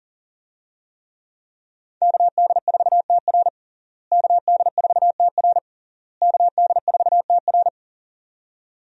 Morse code, heard as a keyed tone, about 40 wpm, 700 Hz. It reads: KD4TR KD4TR KD4TR
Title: USA Call Signs - 5 Characters - Repeated 3x - 40wpm